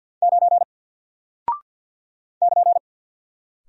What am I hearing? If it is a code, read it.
Morse code, heard as a keyed tone, about 48 wpm, 700 Hz. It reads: 9 <KN>
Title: Single Character (Letter:Number:Punctuation:Prosign) 50wpm